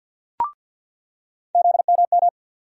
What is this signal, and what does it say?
Morse code, heard as a keyed tone, about 48 wpm, 700 Hz. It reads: ZMM
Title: USA Call Sign Postfixes - 3 Letters 50wpm